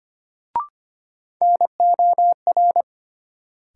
Morse code, heard as a keyed tone, about 25 wpm, 700 Hz. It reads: NOR
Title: Sentences from Top 1500 Words - New Word Drill Alt - 25wpm